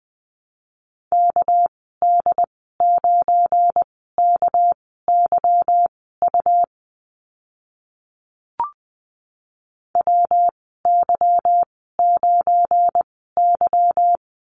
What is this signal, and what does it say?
Morse code, heard as a keyed tone, about 20 wpm, spaced as for 14 wpm, 700 Hz. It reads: KD9KYU WY9Y
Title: 238 Lesson 38g - Letter q - Call Signs 20wpm